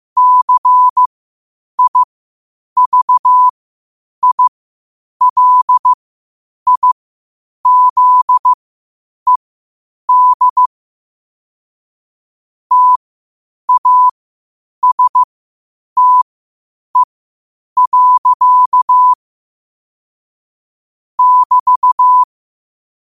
Morse code, heard as a keyed tone, about 15 wpm, spaced as for 8 wpm, 1000 Hz. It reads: CIVILIZED TASTE. <BT>
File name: qotdfiles / 8wpm